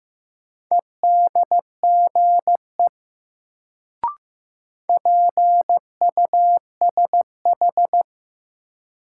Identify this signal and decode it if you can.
Morse code, heard as a keyed tone, about 15 wpm, 700 Hz. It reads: EDGE PUSH
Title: Sentences from Top 1300 Words - New Word Drill Alt - 15wpm